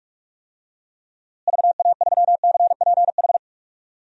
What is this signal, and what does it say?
Morse code, heard as a keyed tone, about 41 wpm, 700 Hz. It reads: VA3CPH
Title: DXCC-100 Call Signs - 6 Characters - Repeated 3x - 45wpm